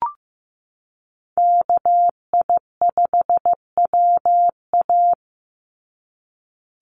Morse code, heard as a keyed tone, about 15 wpm, 700 Hz. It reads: KI5WA
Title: NA SOTA Chaser Callsigns 15wpm